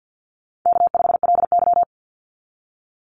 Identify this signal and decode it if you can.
Morse code, heard as a keyed tone, about 50 wpm, 700 Hz. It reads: K5LY
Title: Call Signs Repeated 2x - 50wpm